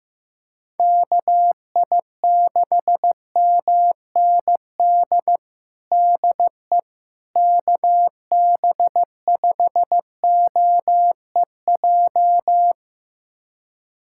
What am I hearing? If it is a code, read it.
Morse code, heard as a keyed tone, about 16 wpm, 700 Hz. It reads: KI6MND DE KB5OEJ
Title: USA Callsign DE Callsign 15wpm